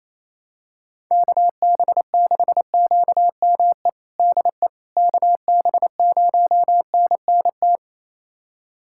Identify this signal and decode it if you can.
Morse code, heard as a keyed tone, about 28 wpm, 700 Hz. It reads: KB6QME DE KB0NNT
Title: USA Callsign DE Callsign - 6 Characters - 28wpm